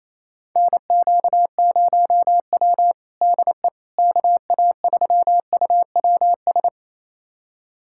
Morse code, heard as a keyed tone, about 28 wpm, 700 Hz. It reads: NQ0W DE KA3UWS
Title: USA Callsign DE Callsign 28wpm